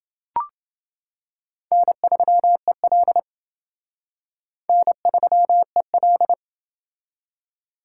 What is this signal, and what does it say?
Morse code, heard as a keyed tone, about 28 wpm, 700 Hz. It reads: N3EL N3EL